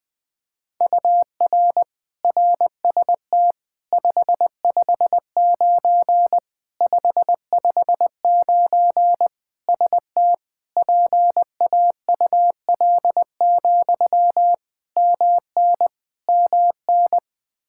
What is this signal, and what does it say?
Morse code, heard as a keyed tone, about 20 wpm, 700 Hz. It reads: UR RST 559 559 ST PAUL, MN MN
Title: Practice QSO - Exchange 1 - Beginning - 20wpm